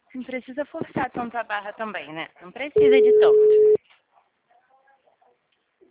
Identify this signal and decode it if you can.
Ringback tone (Europe, 425 Hz), with speech.
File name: out- -2004- - - .40622